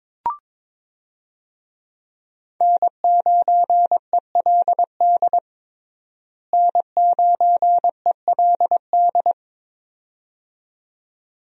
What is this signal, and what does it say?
Morse code, heard as a keyed tone, about 22 wpm, 700 Hz. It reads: N9ELD N9ELD
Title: USA Call Signs - 5 Characters - Repeated 2x - 22wpm